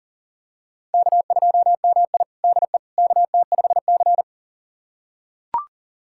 Morse code, heard as a keyed tone, about 40 wpm, 700 Hz. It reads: K2MI DE KT5C
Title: USA Callsign DE Callsign - 4 Characters - 40wpm